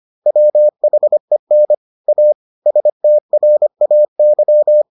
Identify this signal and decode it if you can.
Morse code, heard as a keyed tone, about 25 wpm, 600 Hz. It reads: WHEN A STRAY